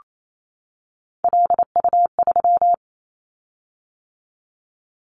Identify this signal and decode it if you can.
Morse code, heard as a keyed tone, about 28 wpm, 700 Hz. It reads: LU3
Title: Callsign Prefixes for Top 50 DXCC - 28wpm